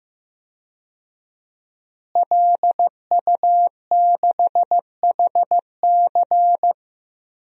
Morse code, heard as a keyed tone, about 15 wpm, 700 Hz. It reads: LU6HC
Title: DXCC-100 Call Signs - 5 Characters - Repeated 2x - 15wpm